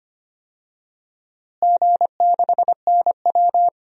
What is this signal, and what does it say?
Morse code, heard as a keyed tone, about 25 wpm, 700 Hz. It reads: G6NW
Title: DXCC-100 Call Signs - 4 Characters - Repeated 2x - 25wpm